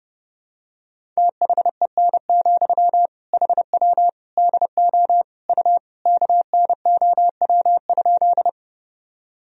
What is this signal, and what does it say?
Morse code, heard as a keyed tone, about 32 wpm, 700 Hz. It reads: THEN, HW DO U KNOW?
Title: Sentences from Top 100 Words and Abbreviations 30wpm